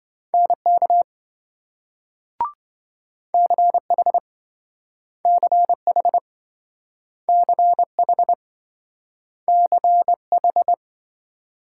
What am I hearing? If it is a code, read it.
Morse code, heard as a keyed tone, about 26 wpm, 700 Hz. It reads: NK CH CH CH CH